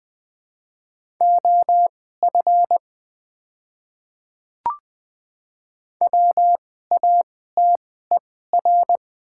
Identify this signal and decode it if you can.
Morse code, heard as a keyed tone, about 20 wpm, spaced as for 15 wpm, 700 Hz. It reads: OF WATER
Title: 2x Character Spacing - Top 100 Words - 20wpm